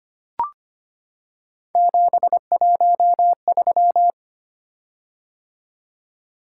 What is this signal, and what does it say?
Morse code, heard as a keyed tone, about 25 wpm, 700 Hz. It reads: 713
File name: Three Letter-Number Combinations - 25wpm